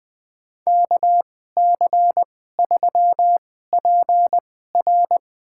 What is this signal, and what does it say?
Morse code, heard as a keyed tone, about 20 wpm, spaced as for 15 wpm, 700 Hz. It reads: KC3PR
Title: USA Call Signs - 5 Characters - 2x Character Spacing - 20wpm